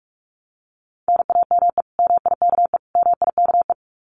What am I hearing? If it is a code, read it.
Morse code, heard as a keyed tone, about 46 wpm, 700 Hz. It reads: NAME MIKE MIKE
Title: Common QSO Phrases Encore 45wpm